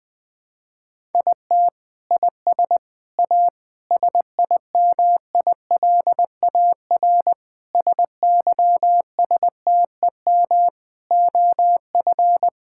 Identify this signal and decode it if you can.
Morse code, heard as a keyed tone, about 21 wpm, 700 Hz. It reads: IT IS A SIMILAR SYSTEM OF
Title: Sentences from Top 1000 Words - Review Part 5 - 20wpm